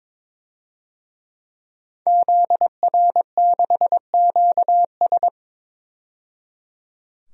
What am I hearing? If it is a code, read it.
Morse code, heard as a keyed tone, about 22 wpm, 700 Hz. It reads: ZR6QS
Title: DXCC-100 Call Signs - 5 Characters - 22wpm